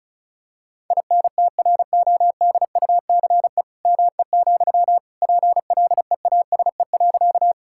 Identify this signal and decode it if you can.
Morse code, heard as a keyed tone, about 36 wpm, 690 Hz. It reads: INTRODUCE ME, PLEASE.
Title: Sentences from Top 1600 Words 35wpm